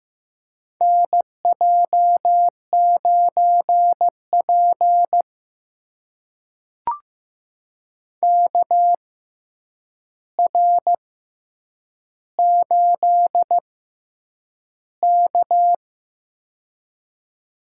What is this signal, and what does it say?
Morse code, heard as a keyed tone, about 15 wpm, spaced as for 8 wpm, 700 Hz. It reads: NJ9P KR8K
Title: USA Call Signs - 4 Characters - 6x Character Spacing - 15wpm